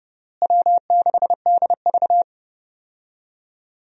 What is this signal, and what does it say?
Morse code, heard as a keyed tone, about 30 wpm, 700 Hz. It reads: W6DV
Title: USA Call Signs - 4 Characters - Repeated 2x - 30wpm